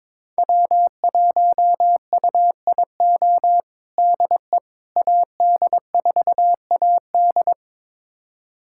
Morse code, heard as a keyed tone, about 21 wpm, 700 Hz. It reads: W1UIO DE AD4AD
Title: USA Callsign DE Callsign - 5 Characters - 22wpm